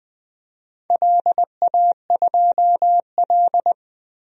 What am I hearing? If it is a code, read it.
Morse code, heard as a keyed tone, about 21 wpm, 700 Hz. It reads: LA2L
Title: DXCC-100 Call Signs - 4 Characters - 20wpm